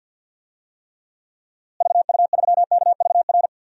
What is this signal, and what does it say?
Morse code, heard as a keyed tone, about 51 wpm, 700 Hz. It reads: VU3KVR
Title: DXCC-100 Call Signs - 6 Characters - Repeated 3x - 50wpm